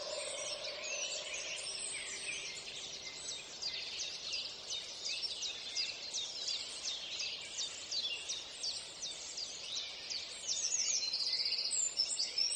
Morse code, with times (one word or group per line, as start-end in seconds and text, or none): none